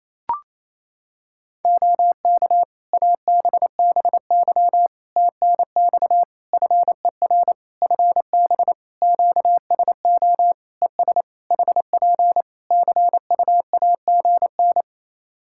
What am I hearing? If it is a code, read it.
Morse code, heard as a keyed tone, about 27 wpm, 700 Hz. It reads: OK ABBY TNX FER FB QSO ES HP CUAGN
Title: Practice QSO - Exchange 4 - Ending - 28wpm